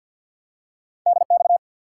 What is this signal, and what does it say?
Morse code, heard as a keyed tone, about 52 wpm, 700 Hz. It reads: DX
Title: Top 90 QSO Elements 50wpm